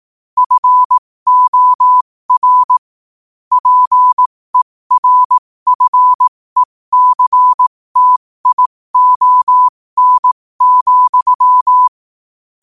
Morse code, heard as a keyed tone, about 18 wpm, spaced as for 15 wpm, 1000 Hz. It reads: FOR PERFECTION,
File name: qotdfiles / 15wpm